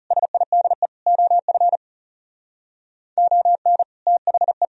Morse code, heard as a keyed tone, about 35 wpm, 700 Hz. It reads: SIDE OF ON THE
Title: Speed-Racing - Sets of 4 Words 30wpm